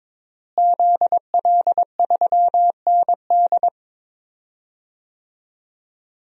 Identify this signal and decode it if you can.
Morse code, heard as a keyed tone, about 22 wpm, 700 Hz. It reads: ZL3ND
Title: DXCC-100 Call Signs - 5 Characters - Repeated 3x - 22wpm